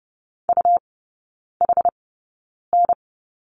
Morse code, heard as a keyed tone, about 30 wpm, spaced as for 9 wpm, 700 Hz. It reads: UHN
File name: Lesson 17e - Letter u - Three Characters 30wpm